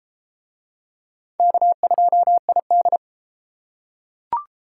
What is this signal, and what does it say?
Morse code, heard as a keyed tone, about 31 wpm, 700 Hz. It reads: K2ID